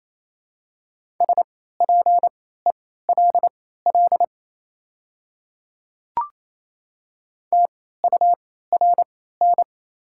Morse code, heard as a keyed tone, about 28 wpm, spaced as for 16 wpm, 700 Hz. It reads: SPELL TURN
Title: 3x Character Spacing - Top 300 Words - 28wpm